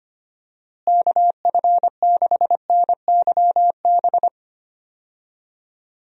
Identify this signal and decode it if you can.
Morse code, heard as a keyed tone, about 25 wpm, 700 Hz. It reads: KF6NYB
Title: USA Call Signs - 6 Characters - Repeated 3x - 25wpm